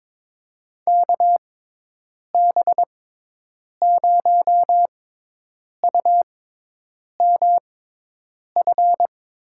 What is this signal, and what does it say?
Morse code, heard as a keyed tone, about 23 wpm, spaced as for 8 wpm, 700 Hz. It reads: KB0UMF
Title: USA Call Signs - 6 Characters - 6x Character Spacing - 22wpm